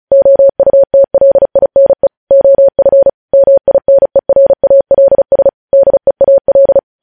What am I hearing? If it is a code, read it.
Morse code, heard as a keyed tone, about 36 wpm, 550 Hz. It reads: OUTLINE OF MINERALS DEAL
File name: podcast / News headlines-1x-35x35